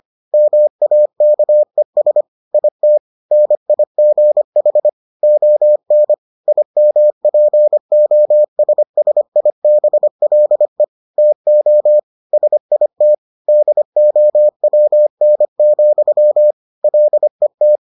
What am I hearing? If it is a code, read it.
Morse code, heard as a keyed tone, about 25 wpm, 600 Hz. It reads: MAKES IT NIGH ON IMPOSSIBLE TO SIT DOWN, LET